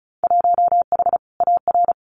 Morse code, heard as a keyed tone, about 36 wpm, 700 Hz. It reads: 1H AR
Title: Winter Field Day Exchanges 35wpm